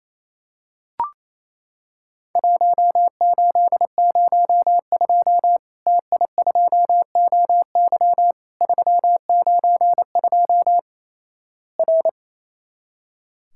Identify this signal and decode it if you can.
Morse code, heard as a keyed tone, about 28 wpm, 700 Hz. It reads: 1802 TI2OY 392 R
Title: Worked All Europe DX Contest - QTC Exchange - 28wpm